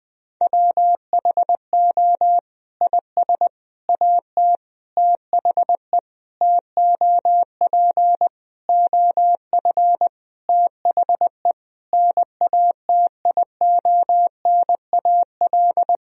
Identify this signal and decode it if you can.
Morse code, heard as a keyed tone, about 21 wpm, 700 Hz. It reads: WHO IS AT THE TOP OF THE NATIONAL
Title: Sentences from Top 1000 Words - Review Part 4 - 20wpm